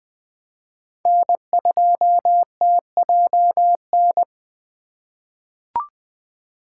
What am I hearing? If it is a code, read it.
Morse code, heard as a keyed tone, about 20 wpm, 700 Hz. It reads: N2TJN
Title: USA Call Signs - 5 Characters - 3x Character Spacing - 20wpm